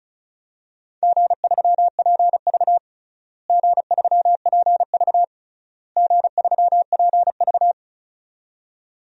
Morse code, heard as a keyed tone, about 36 wpm, 700 Hz. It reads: G3PV G3PV G3PV
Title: DXCC-100 Call Signs - 4 Characters - Repeated 3x - 35wpm